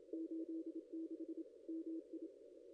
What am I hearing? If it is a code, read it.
Morse code, heard as a keyed tone, about 25 wpm, 320 Hz. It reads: 86 MI
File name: contests / ss-cw-k5tr-op-n5rz -0900